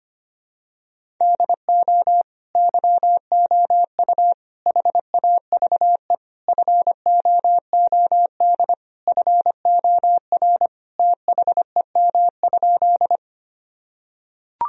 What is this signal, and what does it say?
Morse code, heard as a keyed tone, about 25 wpm, 700 Hz. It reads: DO YOU HAVE FOOD FOR THEM?
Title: Sentences from Top 1000 Words - Review Part 1 - 25wpm